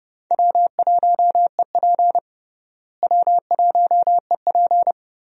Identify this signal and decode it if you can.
Morse code, heard as a keyed tone, about 30 wpm, 700 Hz. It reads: W1EP W1EP
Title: USA Call Signs - 4 Characters - Repeated 3x - 30wpm